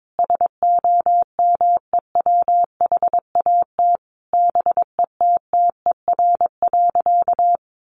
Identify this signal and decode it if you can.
Morse code, heard as a keyed tone, about 22 wpm, 700 Hz. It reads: SOMEWHAT BETTER.
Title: Sentences from Top 1300 Words 22wpm